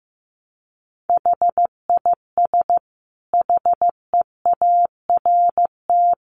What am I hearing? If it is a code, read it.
Morse code, heard as a keyed tone, about 15 wpm, 700 Hz. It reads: HIS HEART
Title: Sentences from Top 1300 Words 15wpm